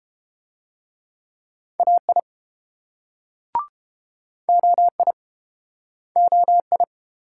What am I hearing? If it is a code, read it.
Morse code, heard as a keyed tone, about 32 wpm, 700 Hz. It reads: AI OI OI